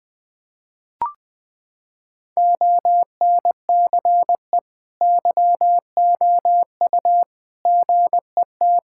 Morse code, heard as a keyed tone, about 21 wpm, 700 Hz. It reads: ONCE YOU GET
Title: Sentences with 7-Word Phrases - 20wpm